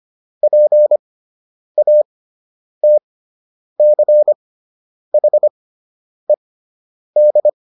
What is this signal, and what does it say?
Morse code, heard as a keyed tone, about 25 wpm, spaced as for 9 wpm, 600 Hz. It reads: PATCHED